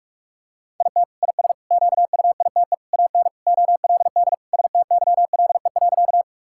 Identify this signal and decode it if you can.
Morse code, heard as a keyed tone, about 46 wpm, 700 Hz. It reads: IT IS QUITE AN OLD STYLE.
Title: Sentences from Top 1000 Words - Review Part 5 - 45wpm